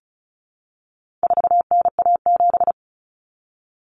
Morse code, heard as a keyed tone, about 34 wpm, 700 Hz. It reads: 4NA7
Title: Common Cut Numbers - 4-Digit 35wpm